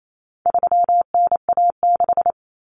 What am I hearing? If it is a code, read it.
Morse code, heard as a keyed tone, about 28 wpm, 700 Hz. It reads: 3NA6
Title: Common Cut Numbers - 4-Digit 28wpm